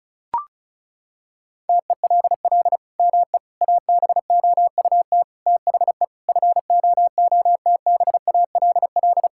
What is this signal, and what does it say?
Morse code, heard as a keyed tone, about 34 wpm, 700 Hz. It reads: TELL ME ABOUT THE FOOTBALL
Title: Sentences from Top 1600 Words 35wpm